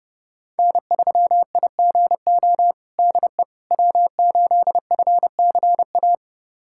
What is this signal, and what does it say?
Morse code, heard as a keyed tone, about 30 wpm, 700 Hz. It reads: N3IGO DE W8FCA
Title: USA Callsign DE Callsign - 5 Characters - 30wpm